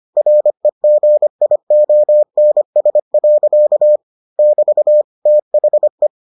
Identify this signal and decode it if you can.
Morse code, heard as a keyed tone, about 25 wpm, 600 Hz. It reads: REGIONS. <BT> THE